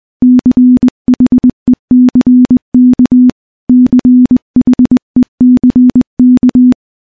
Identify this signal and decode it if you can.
Morse code, heard as a keyed tone, about 20 wpm, 250 Hz. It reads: CHECK CHECK
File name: morse-training-20WPM